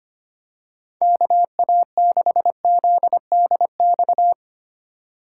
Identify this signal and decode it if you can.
Morse code, heard as a keyed tone, about 25 wpm, 700 Hz. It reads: KA6ZDX
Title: USA Call Signs - 2x Character Spacing - 25wpm